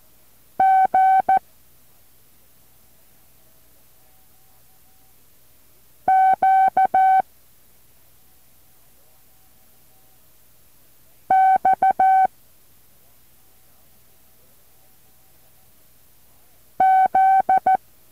Morse code, heard as a keyed tone, about 14 wpm, 760 Hz. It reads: G Q X Z
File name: intl morse code2